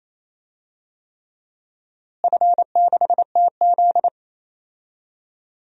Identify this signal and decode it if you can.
Morse code, heard as a keyed tone, about 27 wpm, 700 Hz. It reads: F6TZ
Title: DXCC-100 Call Signs - 4 Characters - Repeated 3x - 28wpm